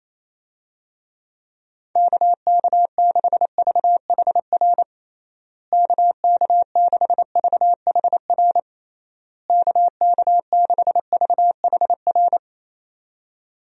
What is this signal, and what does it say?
Morse code, heard as a keyed tone, about 28 wpm, 700 Hz. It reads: KK6VHR KK6VHR KK6VHR
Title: USA Call Signs - 6 Characters - Repeated 3x - 28wpm